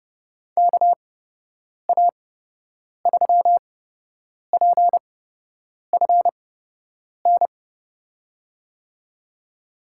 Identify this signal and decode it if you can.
Morse code, heard as a keyed tone, about 32 wpm, spaced as for 8 wpm, 700 Hz. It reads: KA3PFN
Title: USA Call Signs - 8x Character Spacing - 30wpm